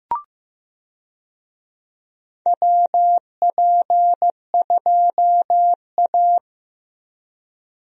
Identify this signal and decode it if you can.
Morse code, heard as a keyed tone, about 15 wpm, 700 Hz. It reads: WP2A
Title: USA Call Signs - 4 Characters - Repeated 3x - 15wpm